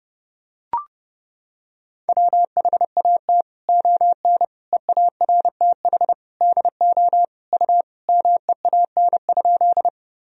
Morse code, heard as a keyed tone, about 32 wpm, 700 Hz. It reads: WHAT ON EARTH DO U MEAN?
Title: Sentences from Top 200 Words and Abbreviations 30wpm